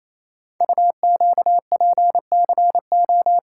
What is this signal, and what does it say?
Morse code, heard as a keyed tone, about 28 wpm, 700 Hz. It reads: UQPCO
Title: Code Groups - 5 Letters - 28wpm